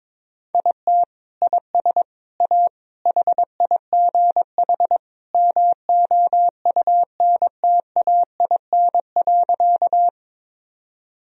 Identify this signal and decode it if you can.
Morse code, heard as a keyed tone, about 22 wpm, 700 Hz. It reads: IT IS A HIGH MOUNTAIN.
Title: Sentences from Top 1000 Words - Review Part 1 - 22wpm